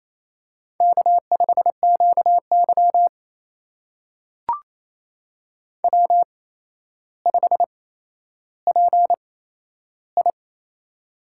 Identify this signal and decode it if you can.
Morse code, heard as a keyed tone, about 27 wpm, spaced as for 12 wpm, 700 Hz. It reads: K5QY W5PI
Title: USA Call Signs - 4 Characters - 8x Character Spacing - 28wpm